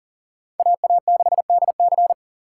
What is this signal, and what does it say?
Morse code, heard as a keyed tone, about 40 wpm, 700 Hz. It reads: AA6DC
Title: USA Call Signs - 5 Characters - 2x Character Spacing - 40wpm